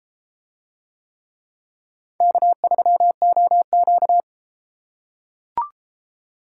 Morse code, heard as a keyed tone, about 32 wpm, 700 Hz. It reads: K3OQ